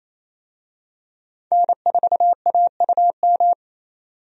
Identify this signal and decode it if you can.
Morse code, heard as a keyed tone, about 27 wpm, 700 Hz. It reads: N4AUM
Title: USA Call Signs - 5 Characters - 28wpm